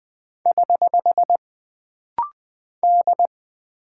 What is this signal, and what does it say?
Morse code, heard as a keyed tone, about 20 wpm, 700 Hz. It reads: <HH> D
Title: Single Character (Letter:Number:Punctuation:Prosign) - Rapid-Fire - 20wpm